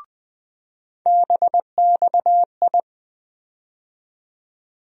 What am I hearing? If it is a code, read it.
Morse code, heard as a keyed tone, about 20 wpm, 700 Hz. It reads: BXI
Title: USA Call Sign Postfixes - 3 Letters 20wpm